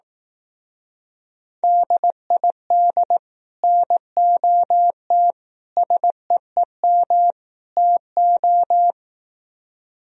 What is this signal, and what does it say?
Morse code, heard as a keyed tone, about 18 wpm, 700 Hz. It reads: DID NOT SEEM TO
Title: Speed-Racing - Sets of 4 Words 15wpm